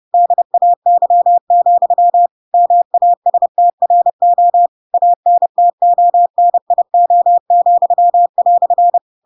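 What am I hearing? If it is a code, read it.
Morse code, heard as a keyed tone, about 30 wpm, 700 Hz. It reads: DAY, MASTRO ANTONIO,"
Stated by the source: Chapter 2 of The Adventures of Pinocchio read as a book in Morse code (CW) at 30 wpm, by ik1hge.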